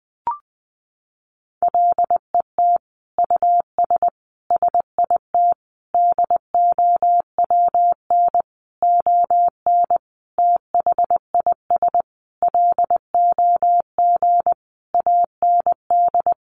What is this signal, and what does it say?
Morse code, heard as a keyed tone, about 20 wpm, 700 Hz. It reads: LET US SIT DOWN ON THIS LOG AND
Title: Sentences from Top 1400 Words 20wpm